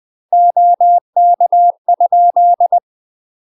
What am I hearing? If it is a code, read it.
Morse code, heard as a keyed tone, about 20 wpm, 700 Hz. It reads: OK?
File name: UK MASTER QSOs TRAINING / 20WPM / UK MASTER QSOs 0025